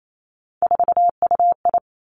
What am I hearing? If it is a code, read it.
Morse code, heard as a keyed tone, about 28 wpm, 700 Hz. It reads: 4UI
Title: Three Challenging Letter-Number Combinations 28wpm